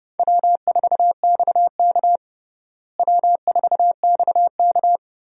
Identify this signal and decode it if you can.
Morse code, heard as a keyed tone, about 31 wpm, 700 Hz. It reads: W4XK W4XK
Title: USA Call Signs - 4 Characters - Repeated 3x - 30wpm